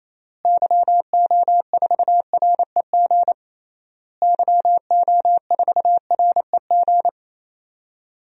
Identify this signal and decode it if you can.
Morse code, heard as a keyed tone, about 28 wpm, 700 Hz. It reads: YO4REG YO4REG
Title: DXCC-100 Call Signs - 6 Characters - Repeated 3x - 28wpm